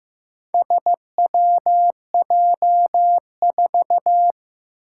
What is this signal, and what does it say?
Morse code, heard as a keyed tone, about 15 wpm, 700 Hz. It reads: SWJ4
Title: Four Challenging Letter-Number Combinations 15wpm